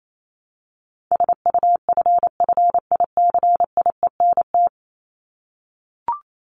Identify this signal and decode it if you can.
Morse code, heard as a keyed tone, about 28 wpm, 700 Hz. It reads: SUFFICIENT
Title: LY Suffix - Top 100 Words with Base Words 28wpm